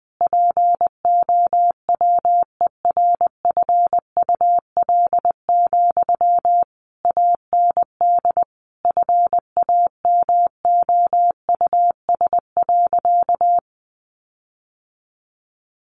Morse code, heard as a keyed tone, about 19 wpm, 700 Hz. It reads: POWERFUL, AND FAMOUS.
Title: Sentences from Top 1500 Words 20wpm